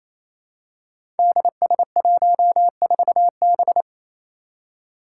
Morse code, heard as a keyed tone, about 27 wpm, 700 Hz. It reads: DS14B
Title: Five Challenging Letter-Number Combinations 28wpm